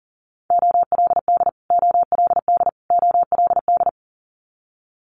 Morse code, heard as a keyed tone, about 39 wpm, 700 Hz. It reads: OLD OLD OLD
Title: Multi-letter Phonemes Repeated 3x 40wpm